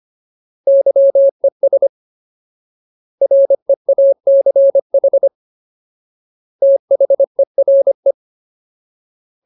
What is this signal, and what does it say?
Morse code, heard as a keyed tone, about 25 wpm, 550 Hz. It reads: YES REACH THERE
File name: Ultimate Word List / 25WPM Ultimate